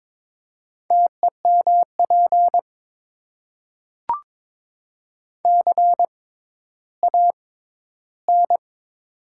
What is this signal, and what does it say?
Morse code, heard as a keyed tone, about 22 wpm, spaced as for 12 wpm, 700 Hz. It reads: TEMP CAN
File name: Top 90 QSO Elements - 6x Character Spacing - 22wpm